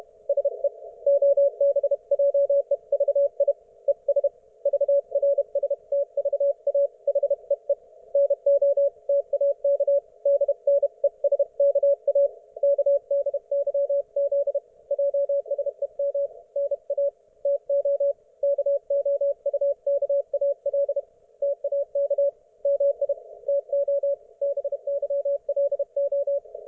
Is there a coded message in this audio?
Yes: Morse code from 0.3 to 26.4 s, heard as a keyed tone, about 32 wpm, 560 Hz. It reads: SE OBJEVI ES VRSTVA HEE NOTAK DNESKA KDYZ JSEM NA TO KOUKAL TAK MI TO BYLO